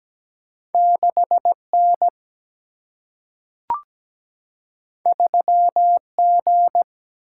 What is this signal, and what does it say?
Morse code, heard as a keyed tone, about 17 wpm, 700 Hz. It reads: 6N 3G
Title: Single-Letter Single-Number 17wpm